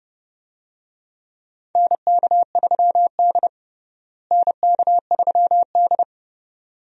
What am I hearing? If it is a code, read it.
Morse code, heard as a keyed tone, about 32 wpm, 700 Hz. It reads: NK3D NK3D
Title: USA Call Signs - 4 Characters - Repeated 2x - 30wpm